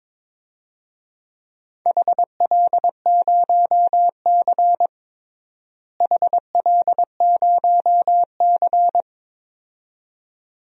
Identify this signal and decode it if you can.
Morse code, heard as a keyed tone, about 22 wpm, 700 Hz. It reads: HL0C HL0C
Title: DXCC-100 Call Signs - 4 Characters - Repeated 2x - 22wpm